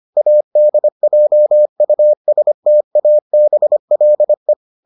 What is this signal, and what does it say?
Morse code, heard as a keyed tone, about 25 wpm, 600 Hz. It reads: ADJUSTABLE